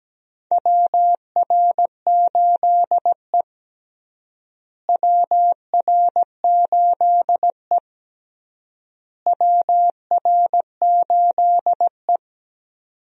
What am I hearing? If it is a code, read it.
Morse code, heard as a keyed tone, about 17 wpm, 700 Hz. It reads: WR8E WR8E WR8E
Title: USA Call Signs - 4 Characters - Repeated 3x - 17wpm